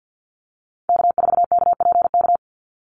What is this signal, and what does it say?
Morse code, heard as a keyed tone, about 50 wpm, 700 Hz. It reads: K4KPK
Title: NA SOTA Activator Callsigns 50wpm